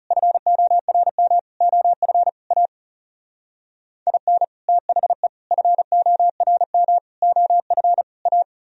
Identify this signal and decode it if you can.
Morse code, heard as a keyed tone, about 38 wpm, 700 Hz. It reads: FORM OF A IN THE FORM OF A